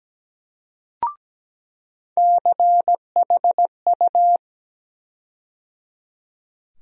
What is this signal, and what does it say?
Morse code, heard as a keyed tone, about 17 wpm, 700 Hz. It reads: CHU
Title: USA Call Sign Postfixes - 3 Letters 17wpm